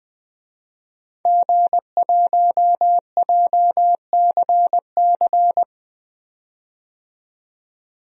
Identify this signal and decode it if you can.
Morse code, heard as a keyed tone, about 20 wpm, 700 Hz. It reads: G1JCC
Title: DXCC-100 Call Signs - 5 Characters - 20wpm